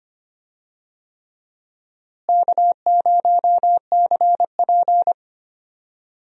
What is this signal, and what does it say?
Morse code, heard as a keyed tone, about 25 wpm, 700 Hz. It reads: K0CP
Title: USA Call Signs - 4 Characters - Repeated 3x - 25wpm